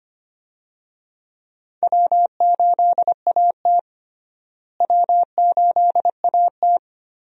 Morse code, heard as a keyed tone, about 25 wpm, 700 Hz. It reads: W8AT W8AT
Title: USA Call Signs - 4 Characters - Repeated 2x - 25wpm